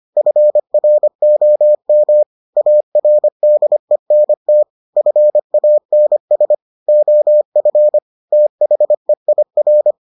Morse code, heard as a keyed tone, about 25 wpm, 600 Hz. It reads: FROM ARDENT FANS OF THEIR